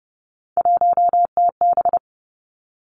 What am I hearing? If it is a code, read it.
Morse code, heard as a keyed tone, about 30 wpm, 700 Hz. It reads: 1TB
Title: Three Challenging Letter-Number Combinations 30wpm